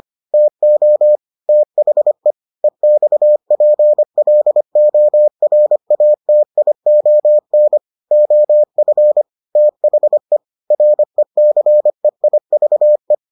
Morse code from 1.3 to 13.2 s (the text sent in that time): THE EXPLORATION OF THE RECEIVE